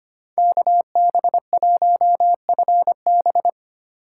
Morse code, heard as a keyed tone, about 25 wpm, 700 Hz. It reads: KB1FB
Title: USA Call Signs - 5 Characters - 4x Character Spacing - 25wpm